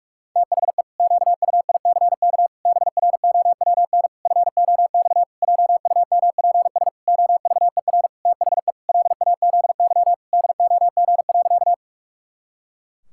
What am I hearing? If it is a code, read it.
Morse code, heard as a keyed tone, about 47 wpm, 700 Hz. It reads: THE QUICK BROWN FOX JUMPS OVER THE LAZY DOG.
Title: Pangram Challenge v2 - 45wpm